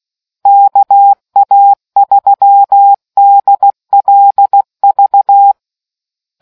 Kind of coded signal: Morse code